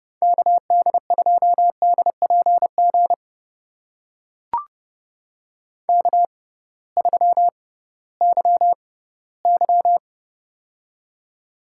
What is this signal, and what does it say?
Morse code, heard as a keyed tone, about 30 wpm, spaced as for 18 wpm, 700 Hz. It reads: KD2DPG K3YY
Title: USA Call Signs - 6x Character Spacing - 30wpm